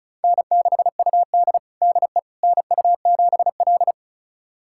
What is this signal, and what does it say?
Morse code, heard as a keyed tone, about 34 wpm, 700 Hz. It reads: N6UD DE NU7L
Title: USA Callsign DE Callsign - 4 Characters - 35wpm